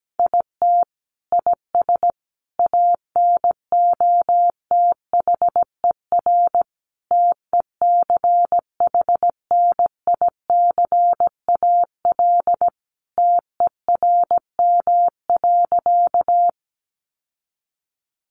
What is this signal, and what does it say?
Morse code, heard as a keyed tone, about 17 wpm, 700 Hz. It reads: IT IS ANOTHER TECHNICAL TERM.
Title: Sentences from Top 1100 Words 17wpm